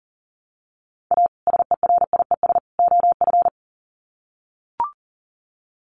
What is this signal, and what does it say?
Morse code, heard as a keyed tone, about 40 wpm, 700 Hz. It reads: A SERIES OF
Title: Sets of 3 Words 40wpm